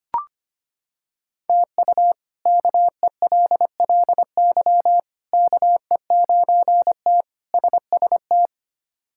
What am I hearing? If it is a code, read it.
Morse code, heard as a keyed tone, about 25 wpm, 700 Hz. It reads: TU KELLY KE9T SST
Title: K1USN SST QSO Practice - 25wpm